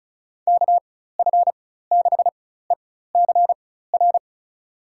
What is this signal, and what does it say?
Morse code, heard as a keyed tone, about 36 wpm, spaced as for 16 wpm, 700 Hz. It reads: KF6ECR
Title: USA Call Signs - 6 Characters - 4x Character Spacing - 35wpm